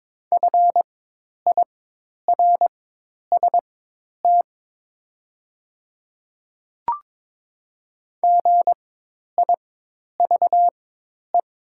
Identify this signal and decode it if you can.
Morse code, heard as a keyed tone, about 22 wpm, spaced as for 10 wpm, 690 Hz. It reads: FIRST GIVE
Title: 4x Character Spacing - Top 200 Words - 22wpm